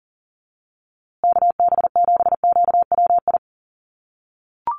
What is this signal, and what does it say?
Morse code, heard as a keyed tone, about 40 wpm, 700 Hz. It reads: KB7QWI
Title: USA Call Signs Repeated 3x 40wpm